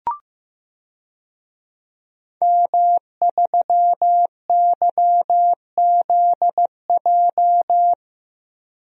Morse code, heard as a keyed tone, about 15 wpm, 700 Hz. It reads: M3YZJ
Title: DXCC-100 Call Signs - 5 Characters - 15wpm